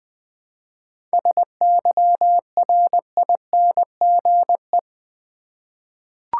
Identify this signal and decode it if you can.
Morse code, heard as a keyed tone, about 21 wpm, 700 Hz. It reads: SYRINGE
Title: Difficult Words with Hints - 20wpm